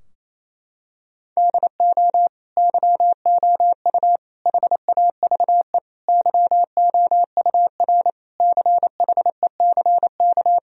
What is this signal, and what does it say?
Morse code, heard as a keyed tone, about 28 wpm, 700 Hz. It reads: DO YOU HAVE YOUR CHECK
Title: Sentences from Top 1000 Words - Review Part 3 - 28wpm